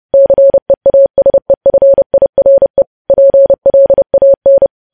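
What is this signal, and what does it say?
Morse code, heard as a keyed tone, about 30 wpm, 550 Hz. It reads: CEASEFIRE PLAN
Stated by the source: Morse news headlines sent at 30 wpm, opening with vvv.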